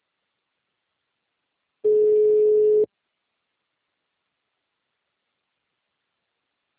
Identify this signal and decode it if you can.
Ringback tone (Europe, 425 Hz).
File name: out- -2001- - - .12